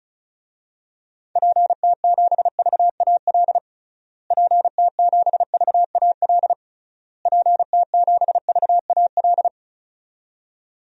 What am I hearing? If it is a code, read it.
Morse code, heard as a keyed tone, about 36 wpm, 700 Hz. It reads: PT7VAL PT7VAL PT7VAL
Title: DXCC-100 Call Signs - 6 Characters - Repeated 3x - 35wpm